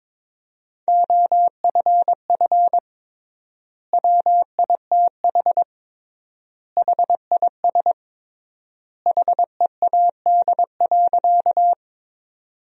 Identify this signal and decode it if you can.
Morse code, heard as a keyed tone, about 22 wpm, 700 Hz. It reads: OFF WITH HIS HEAD.
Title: 3x Word-Spacing - Sentences from Top 300 Words - 22wpm